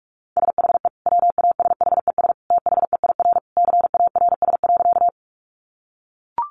Morse code, heard as a keyed tone, about 44 wpm, 700 Hz. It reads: SHE WASHES THEIR CARS.
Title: Easy Phrases 45wpm